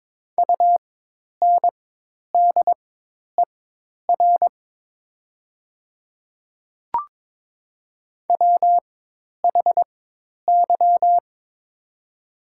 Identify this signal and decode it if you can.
Morse code, heard as a keyed tone, about 22 wpm, spaced as for 10 wpm, 690 Hz. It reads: UNDER WHY
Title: 4x Character Spacing - Top 200 Words - 22wpm